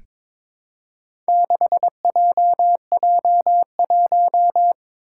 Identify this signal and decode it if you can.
Morse code, heard as a keyed tone, about 22 wpm, 700 Hz. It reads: 6JJ1
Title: Four Challenging Letter-Number Combinations 22wpm